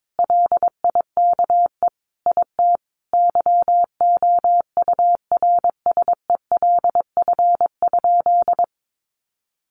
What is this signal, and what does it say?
Morse code, heard as a keyed tone, about 22 wpm, 700 Hz. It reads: LIKE IT YOURSELF?
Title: Sentences from Top 1300 Words 22wpm